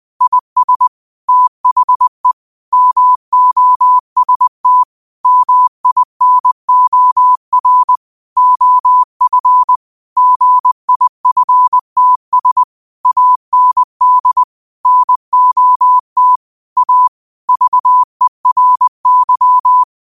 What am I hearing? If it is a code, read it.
Morse code, heard as a keyed tone, about 20 wpm, 1000 Hz. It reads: IS THE MOST MINOR OF GIFTS AND NOT A VERY